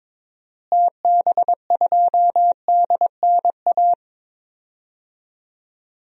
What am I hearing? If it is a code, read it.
Morse code, heard as a keyed tone, about 22 wpm, 700 Hz. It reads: TB2DNA
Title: DXCC-100 Call Signs - 6 Characters - Repeated 2x - 22wpm